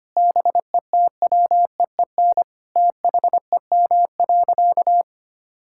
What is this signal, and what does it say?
Morse code, heard as a keyed tone, about 25 wpm, 700 Hz. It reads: BETWEEN THEM.
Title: Sentences from Top 1000 Words - Review Part 6 - 25wpm